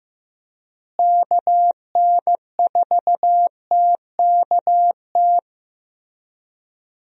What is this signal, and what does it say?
Morse code, heard as a keyed tone, about 15 wpm, 700 Hz. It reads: KN4TKT
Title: USA Call Signs - 6 Characters - Repeated 3x - 15wpm